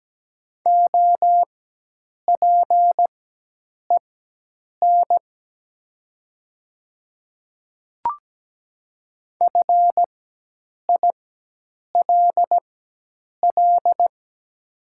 Morse code, heard as a keyed tone, about 17 wpm, spaced as for 8 wpm, 690 Hz. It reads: OPEN FILL
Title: 4x Character Spacing - Top 500 Words - 17wpm